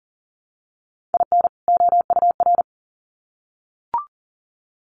Morse code, heard as a keyed tone, about 40 wpm, 700 Hz. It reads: IN OUR
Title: Sets of 2 Words 40wpm